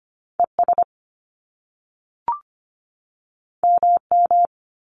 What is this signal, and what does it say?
Morse code, heard as a keyed tone, about 25 wpm, 700 Hz. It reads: ES MM
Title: Top Two Letter Combinations in Words 25wpm